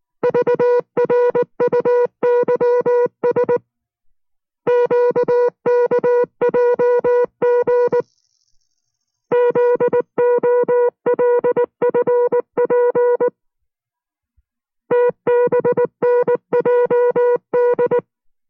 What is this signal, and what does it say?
Morse code, heard as a keyed tone, about 19 wpm, 490 Hz. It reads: VRUYS QKJG ZOLFP TBNJD